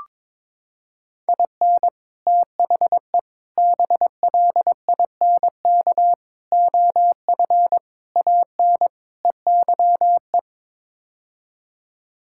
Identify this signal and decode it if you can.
Morse code, heard as a keyed tone, about 22 wpm, 700 Hz. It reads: IN THE BLINK OF AN EYE
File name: practice / 22wpm / Sets of 6 Words - 22wpm